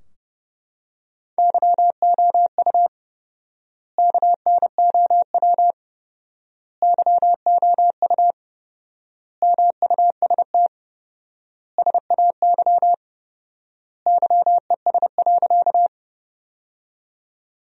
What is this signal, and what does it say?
Morse code, heard as a keyed tone, about 29 wpm, 700 Hz. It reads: YOU KNOW YOU MUST SAY YES.
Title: 4x Word-Spacing - Sentences from Top 500 Words - 30wpm